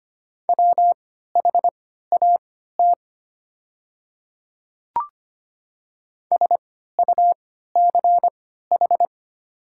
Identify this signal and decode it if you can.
Morse code, heard as a keyed tone, about 25 wpm, spaced as for 14 wpm, 700 Hz. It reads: WHAT SUCH
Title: 3x Character Spacing - Top 200 Words - 25wpm